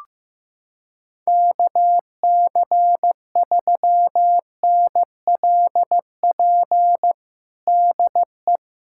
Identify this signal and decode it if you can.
Morse code, heard as a keyed tone, about 15 wpm, 700 Hz. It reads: KC3NLP DE
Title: USA Callsign DE Callsign 15wpm